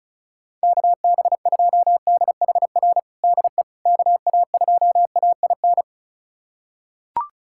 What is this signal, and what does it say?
Morse code, heard as a keyed tone, about 36 wpm, 700 Hz. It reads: KB2DHR DE KA2AIN
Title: USA Callsign DE Callsign - 6 Characters - 35wpm